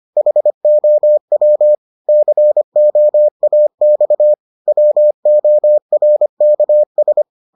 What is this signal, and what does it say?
Morse code, heard as a keyed tone, about 25 wpm, 600 Hz. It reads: HOW COAX WORKS